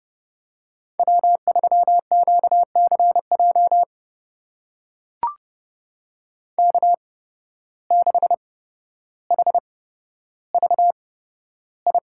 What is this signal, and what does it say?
Morse code, heard as a keyed tone, about 29 wpm, spaced as for 13 wpm, 700 Hz. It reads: W3QCJ K6HVI
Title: USA Call Signs - 5 Characters - 8x Character Spacing - 30wpm